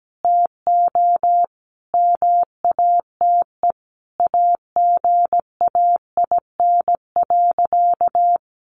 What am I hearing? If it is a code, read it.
Morse code, heard as a keyed tone, about 17 wpm, 700 Hz. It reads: TO MATE AGAIN.
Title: A Sound Of Thunder 17wpm